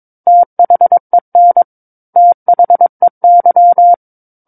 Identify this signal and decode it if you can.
Morse code, heard as a keyed tone, about 22 wpm, 700 Hz. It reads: THEN THEY
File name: 06-Chapter-06 22 WPM